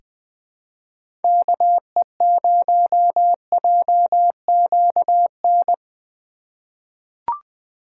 Morse code, heard as a keyed tone, about 20 wpm, 700 Hz. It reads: KE0JQN
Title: USA Call Signs - 6 Characters - 4x Character Spacing - 20wpm